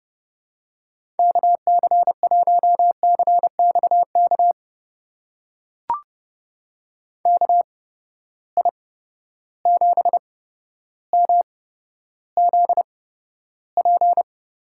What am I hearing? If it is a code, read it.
Morse code, heard as a keyed tone, about 30 wpm, spaced as for 13 wpm, 700 Hz. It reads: KC1CXK KI7MZP
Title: USA Call Signs - 6 Characters - 8x Character Spacing - 30wpm